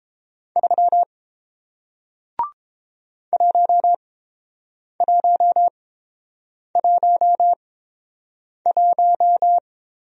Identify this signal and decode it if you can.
Morse code, heard as a keyed tone, about 29 wpm, spaced as for 7 wpm, 700 Hz. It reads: 3 1111